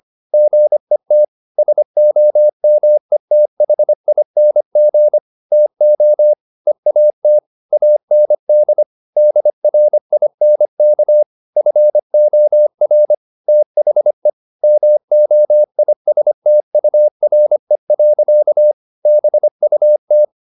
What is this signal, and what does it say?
Morse code, heard as a keyed tone, about 25 wpm, 600 Hz. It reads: GET SOMETHING TO EAT AND DRINK FOR THE MOISTURE. BUT